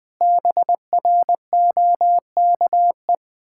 Morse code, heard as a keyed tone, about 19 wpm, 700 Hz. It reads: BROKE
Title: Pangram Challenge v2 - 20wpm